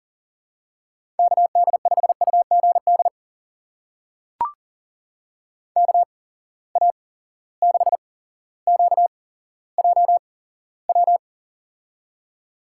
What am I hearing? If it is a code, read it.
Morse code, heard as a keyed tone, about 39 wpm, spaced as for 17 wpm, 700 Hz. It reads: KD5UGD KA6QJW
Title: USA Call Signs - 6 Characters - 8x Character Spacing - 40wpm